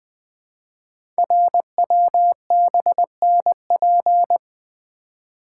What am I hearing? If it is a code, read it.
Morse code, heard as a keyed tone, about 20 wpm, 700 Hz. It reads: RWBNP
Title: Code Groups - 5 Letters - 20wpm